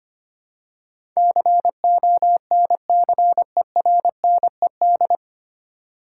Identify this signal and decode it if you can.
Morse code, heard as a keyed tone, about 25 wpm, 700 Hz. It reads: CONCERNED
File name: Nine Letter Words - 25wpm